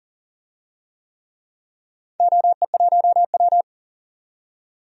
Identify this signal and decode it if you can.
Morse code, heard as a keyed tone, about 40 wpm, 700 Hz. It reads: OE1W
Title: DXCC-100 Call Signs - 4 Characters - 40wpm